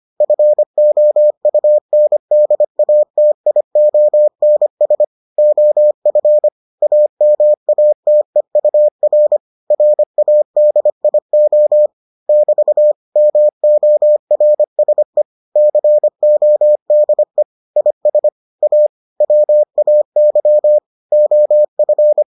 Morse code, heard as a keyed tone, about 25 wpm, 600 Hz. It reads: FOUNDATIONS OF AMATEUR RADIO <BT> MORSE CODE IS A WAY OF